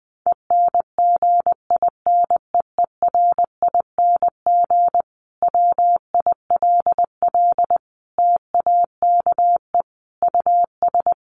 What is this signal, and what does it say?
Morse code, heard as a keyed tone, about 20 wpm, 700 Hz. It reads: ENGINEERING WILL TAKE US
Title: Sentences from Top 1500 Words 20wpm